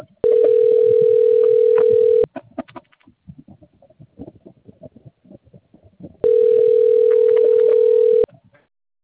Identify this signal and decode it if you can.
Ringback tone (North America), 440 + 480 Hz.